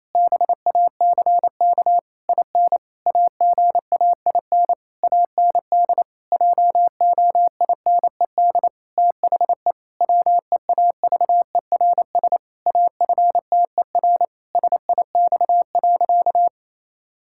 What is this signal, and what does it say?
Morse code, heard as a keyed tone, about 28 wpm, 700 Hz. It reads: BACK IN AGAIN AND JOINED THE WEAVERS AFTER SIX.
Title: Pangram Challenge v2 - 28wpm